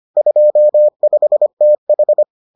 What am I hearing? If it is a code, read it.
Morse code, heard as a keyed tone, about 25 wpm, 600 Hz. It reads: 25TH